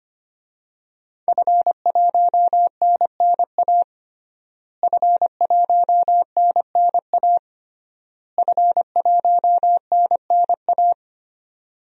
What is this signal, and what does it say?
Morse code, heard as a keyed tone, about 25 wpm, 700 Hz. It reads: F1NNA F1NNA F1NNA
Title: DXCC-100 Call Signs - 5 Characters - Repeated 3x - 25wpm